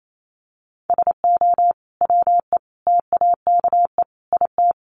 Morse code, heard as a keyed tone, about 28 wpm, 700 Hz. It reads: SO WE TAKE IT
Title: Sentences from Top 700 Words 28wpm